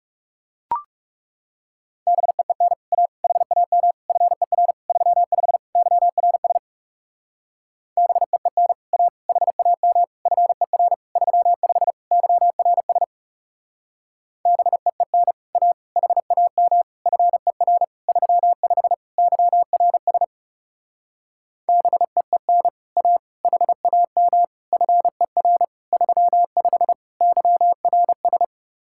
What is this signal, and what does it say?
Morse code, heard as a keyed tone, about 36 wpm, 700 Hz. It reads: BEEN A HAM FER 35 YRS BEEN A HAM FER 35 YRS BEEN A HAM FER 35 YRS BEEN A HAM FER 35 YRS
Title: Speed-Racing Common QSO Phrases - 30wpm